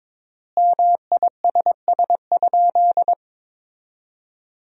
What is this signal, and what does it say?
Morse code, heard as a keyed tone, about 22 wpm, 700 Hz. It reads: MISS?
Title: 3x Word-Spacing - Sentences from Top 500 Words - 22wpm